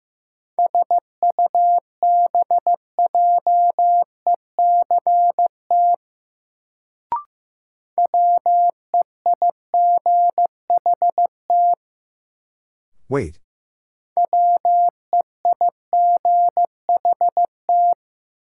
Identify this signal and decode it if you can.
Morse code, heard as a keyed tone, about 15 wpm, 700 Hz. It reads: SUBJECT WEIGHT WEIGHT
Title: Sentences from Top 800 Words - New Word Drill - 15wpm